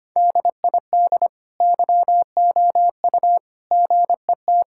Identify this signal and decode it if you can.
Morse code, heard as a keyed tone, about 25 wpm, 700 Hz. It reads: DID YOU GET
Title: Sentences from Top 1000 Words - Review Part 5 - 25wpm